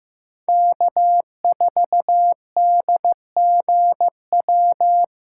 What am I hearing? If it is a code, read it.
Morse code, heard as a keyed tone, about 15 wpm, 700 Hz. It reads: K4DGW
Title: USA Callsign DE Callsign 15wpm